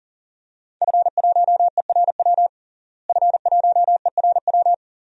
Morse code, heard as a keyed tone, about 40 wpm, 700 Hz. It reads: F1ERW F1ERW
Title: DXCC-100 Call Signs - 5 Characters - Repeated 2x - 40wpm